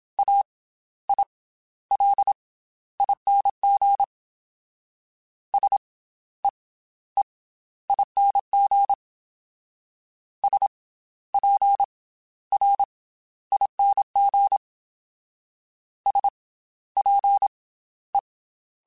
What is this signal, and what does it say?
Morse code, heard as a keyed tone, about 27 wpm, 800 Hz. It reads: A I L ING S E E ING S P R ING S P E
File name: Special INGs / Special INGs-mp3s / Slow ing @25